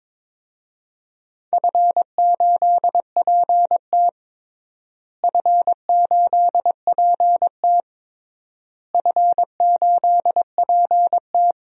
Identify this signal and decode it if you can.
Morse code, heard as a keyed tone, about 22 wpm, 700 Hz. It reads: F8PT F8PT F8PT
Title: DXCC-100 Call Signs - 4 Characters - Repeated 3x - 22wpm